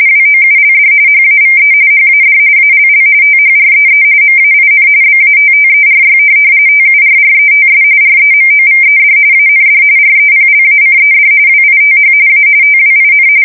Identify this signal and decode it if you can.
RTTY, about 45.45 baud. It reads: WOULD HAVE HAD ALL THREE MAIN BATTERY TURRETS REMOVED AND REPLACED WITH A TWIN
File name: fsk170 / USS-Massachusetts